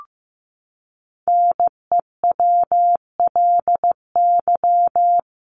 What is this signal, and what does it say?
Morse code, heard as a keyed tone, about 15 wpm, 700 Hz. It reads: NEWLY
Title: LY Suffix - Top 100 Words with Base Words 15wpm